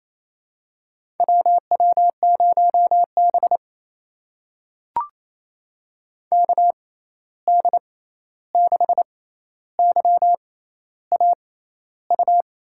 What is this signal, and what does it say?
Morse code, heard as a keyed tone, about 28 wpm, spaced as for 13 wpm, 700 Hz. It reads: WW0B KD6YAU
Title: USA Call Signs - 6x Character Spacing - 28wpm